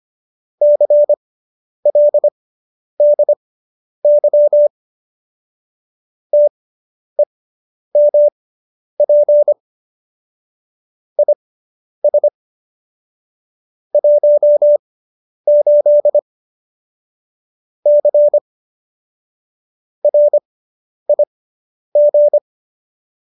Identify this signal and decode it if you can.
Morse code, heard as a keyed tone, about 25 wpm, spaced as for 10 wpm, 590 Hz. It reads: CLDY TEMP IS 18 C RIG